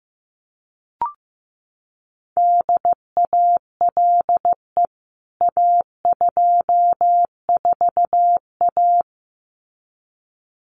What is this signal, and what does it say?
Morse code, heard as a keyed tone, about 15 wpm, 700 Hz. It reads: DALE A24A
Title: CWT Exchange 15wpm